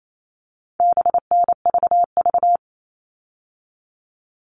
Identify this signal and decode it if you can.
Morse code, heard as a keyed tone, about 28 wpm, 700 Hz. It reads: BNVV
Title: All Cut Numbers - 4-Digit 28wpm